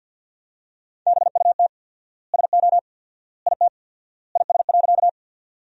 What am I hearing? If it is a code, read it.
Morse code, heard as a keyed tone, about 50 wpm, 700 Hz. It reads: BUT SO IT IS.
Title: 4x Word-Spacing - Sentences from Top 100 Words - 50wpm